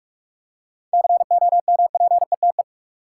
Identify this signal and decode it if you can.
Morse code, heard as a keyed tone, about 46 wpm, 700 Hz. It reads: COMPETE
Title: Seven Letter Words - 45wpm